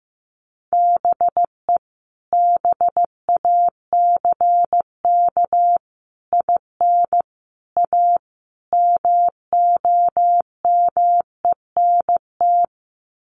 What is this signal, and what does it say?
Morse code, heard as a keyed tone, about 15 wpm, 700 Hz. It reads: BE BACK IN A MOMENT
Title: Sets of 5 Words 15wpm